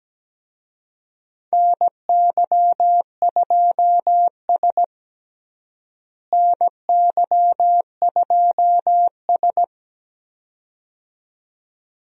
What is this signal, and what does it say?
Morse code, heard as a keyed tone, about 17 wpm, 700 Hz. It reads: NY2S NY2S
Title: USA Call Signs - 4 Characters - Repeated 2x - 17wpm